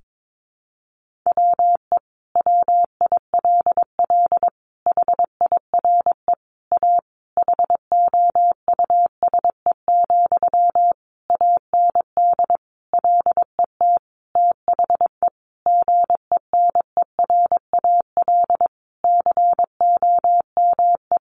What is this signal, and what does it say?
Morse code, heard as a keyed tone, about 22 wpm, 700 Hz. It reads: WE WILL HIRE A HOUSE, AND LET THE GENERAL COME
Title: Sentences from Top 1400 Words 22wpm